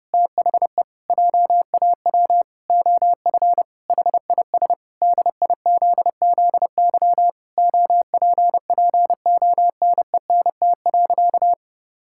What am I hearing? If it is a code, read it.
Morse code, heard as a keyed tone, about 30 wpm, 700 Hz. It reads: THE JAW OF HIS DIZZY OPPONENT.
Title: Pangram Challenge v2 - 30wpm